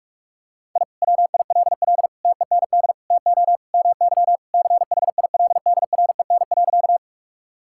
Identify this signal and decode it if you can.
Morse code, heard as a keyed tone, about 46 wpm, 690 Hz. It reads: I WILL TEND TO MY CHILDREN.
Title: Sentences from Top 1000 Words - Review Part 5 - 45wpm